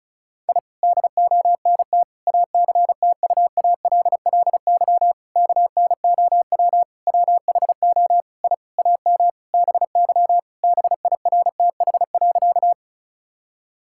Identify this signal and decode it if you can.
Morse code, heard as a keyed tone, about 36 wpm, 690 Hz. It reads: I DONT ACTUALLY KNOW WHO I AM BY BIRTH.
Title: Sentences from Top 1000 Words - Review Part 5 - 35wpm